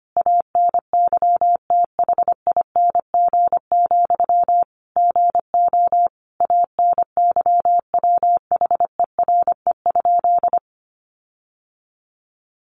Morse code, heard as a keyed tone, about 25 wpm, 700 Hz. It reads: ANYTHING, GO ANYWHERE?
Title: Sentences from Top 1400 Words 25wpm